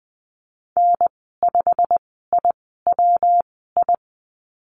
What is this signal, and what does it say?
Morse code, heard as a keyed tone, about 20 wpm, spaced as for 15 wpm, 700 Hz. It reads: N5IWI
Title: 147 Lesson 24g - Letter y - Call Signs 20wpm